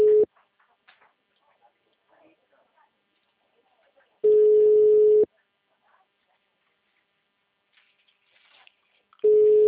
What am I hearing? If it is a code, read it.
Ringback tone (Europe, 425 Hz), with speech.